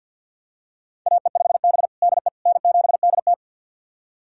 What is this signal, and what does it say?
Morse code, heard as a keyed tone, about 51 wpm, 700 Hz. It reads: AE5B DE N7DT
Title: USA Callsign DE Callsign - 4 Characters - 50wpm